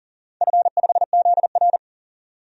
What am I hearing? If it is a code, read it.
Morse code, heard as a keyed tone, about 40 wpm, 700 Hz. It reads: F5ZR
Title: DXCC-100 Call Signs - 4 Characters - 40wpm